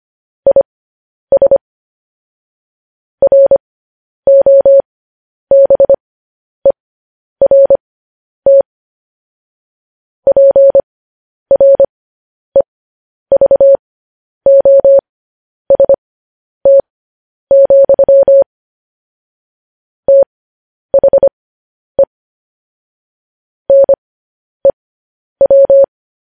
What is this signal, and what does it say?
Morse code, heard as a keyed tone, about 25 wpm, spaced as for 10 wpm, 550 Hz. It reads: IS ROBERT PREVOST, THE NEW
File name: podcast / News headlines-1x-25x10